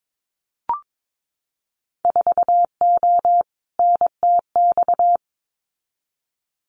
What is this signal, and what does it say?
Morse code, heard as a keyed tone, about 22 wpm, 700 Hz. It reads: 4O NTX
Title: Winter Field Day Exchanges 22wpm